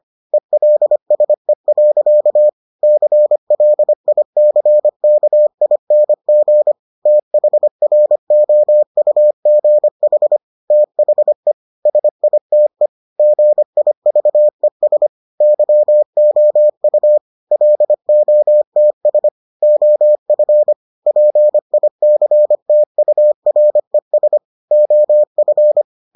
Morse code, heard as a keyed tone, about 25 wpm, 600 Hz. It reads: ELSE. CLICKING THROUGH THE SITE GIVES YOU LOTS OF PICTURES OF